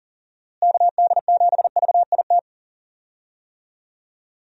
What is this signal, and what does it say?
Morse code, heard as a keyed tone, about 40 wpm, 700 Hz. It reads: KD7VIT
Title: USA Call Signs - 6 Characters - Repeated 2x - 40wpm